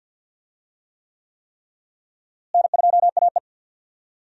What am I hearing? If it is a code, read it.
Morse code, heard as a keyed tone, about 48 wpm, 700 Hz. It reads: N2AE
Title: USA Call Signs - 4 Characters - Repeated 3x - 50wpm